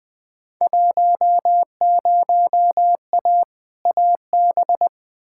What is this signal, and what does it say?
Morse code, heard as a keyed tone, about 20 wpm, 700 Hz. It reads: 10A AB
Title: Field Day Exchanges Encore - 20wpm